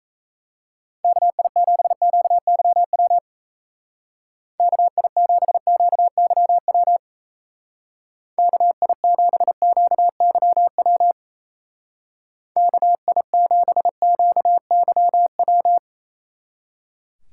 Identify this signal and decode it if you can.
Morse code, heard as a keyed tone, about 36 wpm, 700 Hz. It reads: KI7QYW KI7QYW KI7QYW KI7QYW